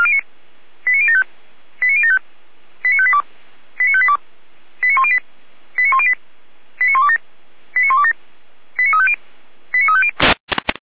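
Five-tone selective call (ZVEI), heard as quick runs of five tones, tones of about 70 ms. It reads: 89075 89075 89582 89582 89108 89108 89127 89127 89350 89350
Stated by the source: ZVEI five-tone selective call received by SDR.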